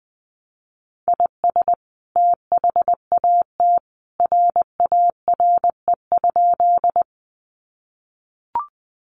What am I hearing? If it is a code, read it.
Morse code, heard as a keyed tone, about 20 wpm, 700 Hz. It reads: IS THAT RARE?
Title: Sentences from Top 1100 Words 20wpm